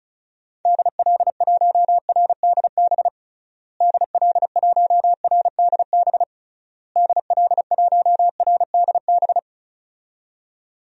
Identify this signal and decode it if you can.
Morse code, heard as a keyed tone, about 34 wpm, 700 Hz. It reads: DL1RDB DL1RDB DL1RDB
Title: DXCC-100 Call Signs - 6 Characters - Repeated 3x - 35wpm